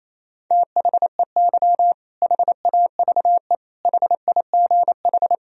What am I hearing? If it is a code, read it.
Morse code, heard as a keyed tone, about 28 wpm, 700 Hz. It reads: THEY HAVE HIGH
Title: Sentences from Top 1000 Words - Review Part 4 - 28wpm